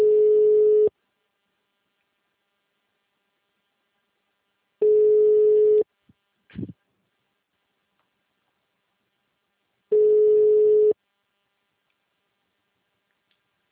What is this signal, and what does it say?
Ringback tone (Europe, 425 Hz).